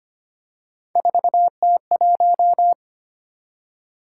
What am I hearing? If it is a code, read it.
Morse code, heard as a keyed tone, about 25 wpm, 700 Hz. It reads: 4T1
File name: Three Letter-Number Combinations - 25wpm